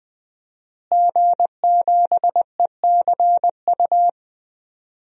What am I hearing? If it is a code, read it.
Morse code, heard as a keyed tone, about 20 wpm, 700 Hz. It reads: G7ECU
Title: DXCC-100 Call Signs - 5 Characters - Repeated 3x - 20wpm